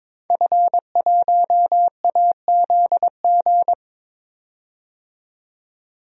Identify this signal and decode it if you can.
Morse code, heard as a keyed tone, about 22 wpm, 700 Hz. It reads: F1AZG
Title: DXCC-100 Call Signs - 5 Characters - 22wpm